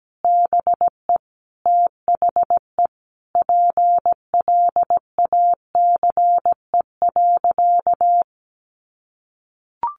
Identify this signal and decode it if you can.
Morse code, heard as a keyed tone, about 17 wpm, 700 Hz. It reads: BE THE PLACE.
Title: Sentences from Top 1300 Words 17wpm